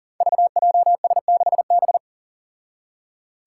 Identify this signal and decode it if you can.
Morse code, heard as a keyed tone, about 40 wpm, 700 Hz. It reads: VJS6B
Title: Five Challenging Letter-Number Combinations 40wpm